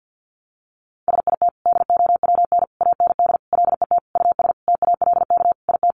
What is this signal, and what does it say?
Morse code, heard as a keyed tone, about 50 wpm, 700 Hz. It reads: SIT DOWN AND LET US TALK IT
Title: Sentences from Top 500 Words 50wpm